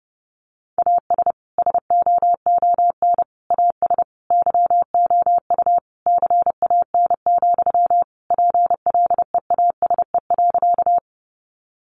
Morse code, heard as a keyed tone, about 30 wpm, 700 Hz. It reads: AS SOON AS YOU CAN, PLEASE.
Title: Sentences from Top 1500 Words 30wpm